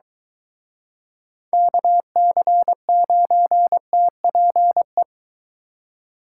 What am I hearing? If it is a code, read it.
Morse code, heard as a keyed tone, about 22 wpm, 700 Hz. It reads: KC9TPE